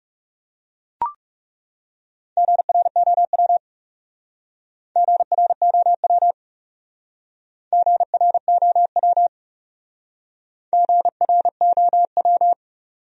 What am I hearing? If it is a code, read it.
Morse code, heard as a keyed tone, about 37 wpm, 700 Hz. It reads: GROW GROW GROW GROW
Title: Speed-Racing Top 500 Words 30wpm